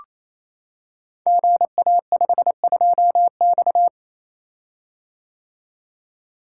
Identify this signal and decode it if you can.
Morse code, heard as a keyed tone, about 29 wpm, 700 Hz. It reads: GA52X
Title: Code Groups - 5 Letter:Numbers - 28wpm